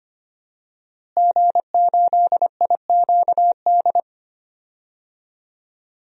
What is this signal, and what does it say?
Morse code, heard as a keyed tone, about 25 wpm, 700 Hz. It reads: G8IQD
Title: DXCC-100 Call Signs - 5 Characters - Repeated 2x - 25wpm